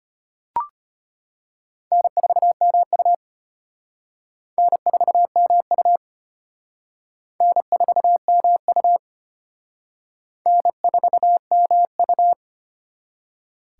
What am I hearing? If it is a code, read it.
Morse code, heard as a keyed tone, about 31 wpm, 700 Hz. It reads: N4MU N4MU N4MU N4MU
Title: Speed Racing - USA 1x2 Call Signs 25wpm